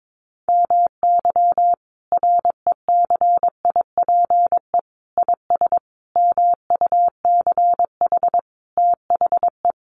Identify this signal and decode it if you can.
Morse code, heard as a keyed tone, about 22 wpm, 700 Hz. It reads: MY RECIPE IS MUCH THE
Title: Sentences from Top 1200 Words 22wpm